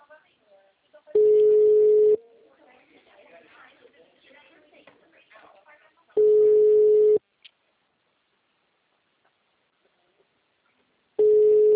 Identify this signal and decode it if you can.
Ringback tone (Europe, 425 Hz), with speech.